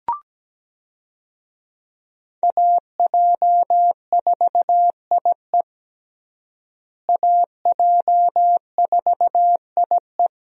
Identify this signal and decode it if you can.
Morse code, heard as a keyed tone, about 17 wpm, 700 Hz. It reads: AJ4IE AJ4IE
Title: USA Call Signs - 5 Characters - Repeated 3x - 17wpm